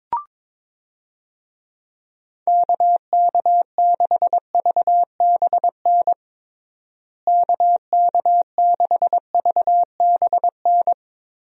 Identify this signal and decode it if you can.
Morse code, heard as a keyed tone, about 23 wpm, 700 Hz. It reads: KK6VBN KK6VBN
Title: USA Call Signs - 6 Characters - Repeated 3x - 22wpm